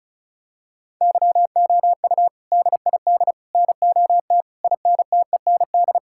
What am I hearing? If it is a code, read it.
Morse code, heard as a keyed tone, about 34 wpm, 700 Hz. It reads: YOU DID NOT INTEND
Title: Sentences from Top 1600 Words 35wpm